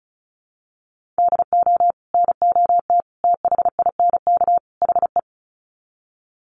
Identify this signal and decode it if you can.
Morse code, heard as a keyed tone, about 34 wpm, 700 Hz. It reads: DO NOT THINK HE
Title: Sets of 4 Words 35wpm